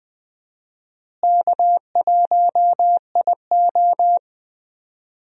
Morse code, heard as a keyed tone, about 20 wpm, 700 Hz. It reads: K1IO
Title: USA Call Signs - 4 Characters - Repeated 2x - 20wpm